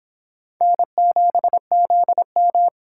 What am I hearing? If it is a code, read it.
Morse code, heard as a keyed tone, about 25 wpm, 700 Hz. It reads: N7ZM